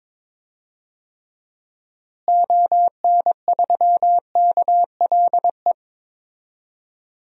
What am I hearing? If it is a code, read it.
Morse code, heard as a keyed tone, about 22 wpm, 700 Hz. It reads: ON3KLE
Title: DXCC-100 Call Signs - 6 Characters - Repeated 2x - 22wpm